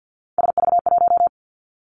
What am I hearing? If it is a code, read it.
Morse code, heard as a keyed tone, about 50 wpm, 700 Hz. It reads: SV1
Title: Callsign Prefixes for Top 50 DXCC - 50wpm